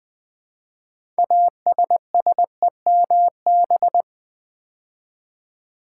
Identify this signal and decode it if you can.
Morse code, heard as a keyed tone, about 20 wpm, 700 Hz. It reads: ASSEMB
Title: Build A Word - 2 Letters At A Time - 8-Letter Words - 20wpm